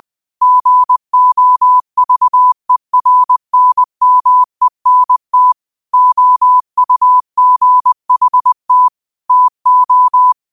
Morse code, heard as a keyed tone, about 20 wpm, 1000 Hz. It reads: GOVERNMENT OUGHT TO